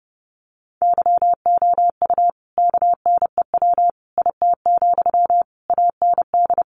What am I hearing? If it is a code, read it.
Morse code, heard as a keyed tone, about 30 wpm, 700 Hz. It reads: YOU KNEW IT, AND
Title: Sentences from Top 500 Words 30wpm